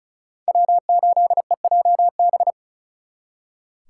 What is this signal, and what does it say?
Morse code, heard as a keyed tone, about 43 wpm, 700 Hz. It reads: W8EJB
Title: Speed Racing - USA 1x3 Call Signs 35wpm